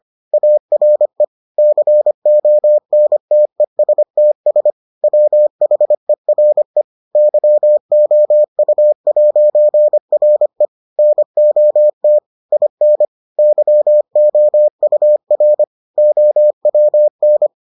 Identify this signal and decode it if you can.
Morse code, heard as a keyed tone, about 25 wpm, 600 Hz. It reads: ARE CONTESTS WHERE YOU'RE NOT IN YOUR OWN